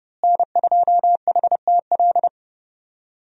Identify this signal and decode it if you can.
Morse code, heard as a keyed tone, about 30 wpm, 700 Hz. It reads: N2HTL
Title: USA Call Signs - 5 Characters - Repeated 3x - 30wpm